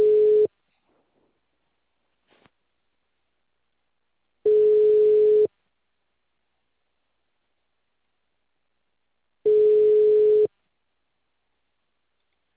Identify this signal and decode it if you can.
Ringback tone (Europe, 425 Hz).